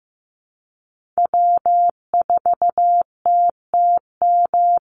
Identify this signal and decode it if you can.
Morse code, heard as a keyed tone, about 15 wpm, 700 Hz. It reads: W4TTM
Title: Call Signs Repeated 3x - 15wpm